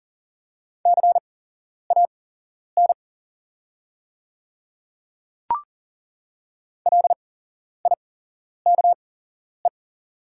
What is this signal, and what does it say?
Morse code, heard as a keyed tone, about 42 wpm, spaced as for 11 wpm, 690 Hz. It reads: CAN LIKE
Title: 8x Character Spacing - Top 100 Words - 40wpm